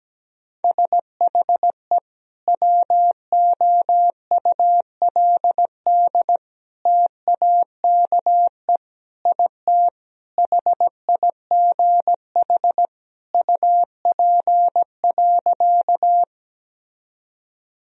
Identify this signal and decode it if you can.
Morse code, heard as a keyed tone, about 17 wpm, 700 Hz. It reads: SHE WOULD TAKE IT HIGH UP.
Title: Sentences from Top 1000 Words - Review Part 1 - 17wpm